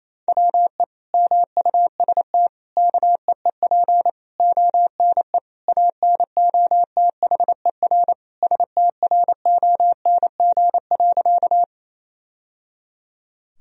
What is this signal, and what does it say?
Morse code, heard as a keyed tone, about 28 wpm, 700 Hz. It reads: WE MUST KEEP ONE ANOTHER STRONG.
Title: Sentences from Top 1000 Words - Review Part 3 - 28wpm